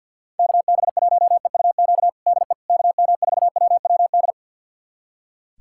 Morse code, heard as a keyed tone, about 47 wpm, 700 Hz. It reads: KB1EUQ DE KM4WWD
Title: USA Callsign DE Callsign - 6 Characters - 50wpm